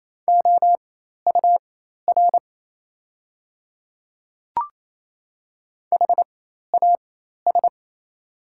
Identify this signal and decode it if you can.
Morse code, heard as a keyed tone, about 27 wpm, spaced as for 13 wpm, 700 Hz. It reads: OUR HAS
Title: 4x Character Spacing - Top 300 Words - 28wpm